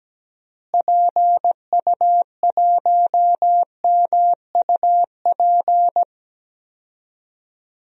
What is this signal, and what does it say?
Morse code, heard as a keyed tone, about 17 wpm, 700 Hz. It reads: PU1MUP
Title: DXCC-100 Call Signs - 6 Characters - Repeated 2x - 17wpm